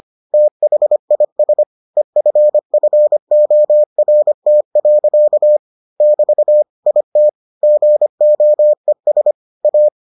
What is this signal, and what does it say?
Morse code, heard as a keyed tone, about 25 wpm, 600 Hz. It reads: THIS EFFORT. <BT> IT GOES A